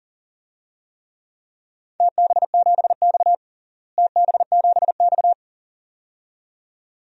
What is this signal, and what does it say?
Morse code, heard as a keyed tone, about 40 wpm, 700 Hz. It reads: TB7X TB7X
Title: DXCC-100 Call Signs - 4 Characters - Repeated 2x - 40wpm